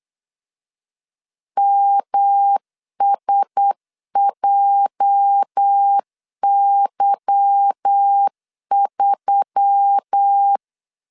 Morse code, heard as a keyed tone, about 8 wpm, 800 Hz. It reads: MSJY3
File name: LEZIONI / 8 YJ45 / 8 YJ 8wpm 3313